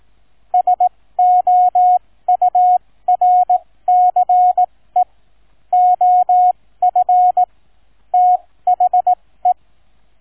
Morse code, heard as a keyed tone, about 18 wpm, spaced as for 15 wpm, 730 Hz. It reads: SOURCE OF THE